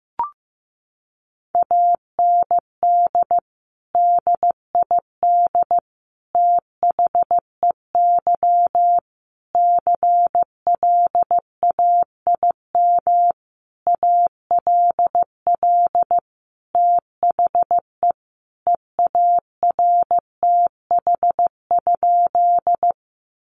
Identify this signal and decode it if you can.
Morse code, heard as a keyed tone, about 15 wpm, 700 Hz. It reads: AND DID THEY CLAIM ALL THE EARTH?
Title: Sentences from Top 1500 Words 15wpm